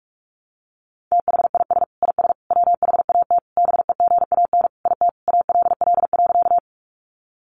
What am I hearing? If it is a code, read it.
Morse code, heard as a keyed tone, about 44 wpm, 700 Hz. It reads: THIS IS WHAT BEGAN IT ALL.
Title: Sentences from Top 300 Words 45wpm